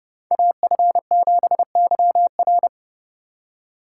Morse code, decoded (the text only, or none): AF7YR